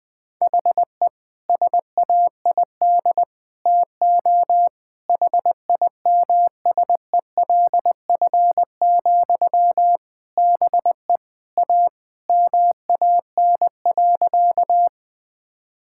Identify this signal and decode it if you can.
Morse code, heard as a keyed tone, about 20 wpm, 700 Hz. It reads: HE SAID TO HIMSELF, BE A MAN.
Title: Sentences from Top 1600 Words 20wpm